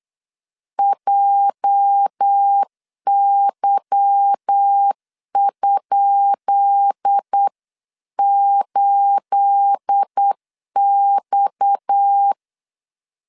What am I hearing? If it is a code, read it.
Morse code, heard as a keyed tone, about 8 wpm, 800 Hz. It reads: JY?8X
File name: LEZIONI / 10 SIGNS / 10B INT 8wpm 3534